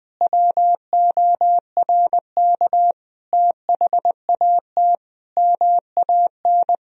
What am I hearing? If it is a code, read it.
Morse code, heard as a keyed tone, about 20 wpm, 700 Hz. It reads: WORK THAT MAN
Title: Sentences from Top 1000 Words - Review Part 2 - 20wpm